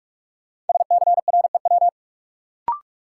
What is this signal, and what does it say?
Morse code, heard as a keyed tone, about 46 wpm, 700 Hz. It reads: SCREW
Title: Sentences from Top 1600 Words - New Word Drill - 45wpm